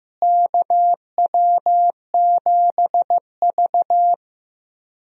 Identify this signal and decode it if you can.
Morse code, heard as a keyed tone, about 15 wpm, 700 Hz. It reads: KW7V
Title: USA Call Signs - 4 Characters - Repeated 2x - 15wpm